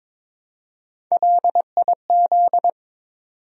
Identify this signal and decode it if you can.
Morse code, heard as a keyed tone, about 22 wpm, 700 Hz. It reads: LIZ
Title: USA Common Names - Max 3 Letters - 22wpm